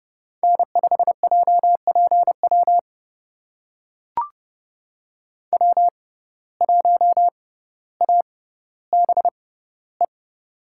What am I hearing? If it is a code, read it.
Morse code, heard as a keyed tone, about 31 wpm, spaced as for 16 wpm, 700 Hz. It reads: N5JPW W1ABE
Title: USA Call Signs - 5 Characters - 6x Character Spacing - 30wpm